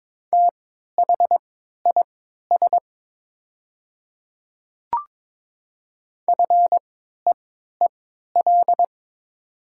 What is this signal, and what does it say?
Morse code, heard as a keyed tone, about 22 wpm, spaced as for 12 wpm, 700 Hz. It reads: THIS FEEL
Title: 3x Character Spacing - Top 500 Words - 22wpm